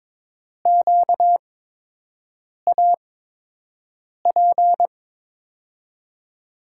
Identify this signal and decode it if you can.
Morse code, heard as a keyed tone, about 22 wpm, spaced as for 6 wpm, 700 Hz. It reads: QAP
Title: USA Call Signs - 6 Characters - 8x Character Spacing - 22wpm